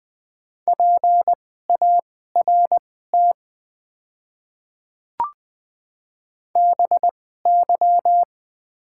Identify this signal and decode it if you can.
Morse code, heard as a keyed tone, about 21 wpm, spaced as for 15 wpm, 700 Hz. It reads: PART BY
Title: 2x Character Spacing - Top 300 Words - 20wpm